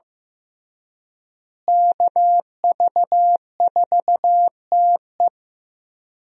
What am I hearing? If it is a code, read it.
Morse code, heard as a keyed tone, about 15 wpm, 700 Hz. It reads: KV4TE
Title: USA Call Signs - 5 Characters - Repeated 2x - 15wpm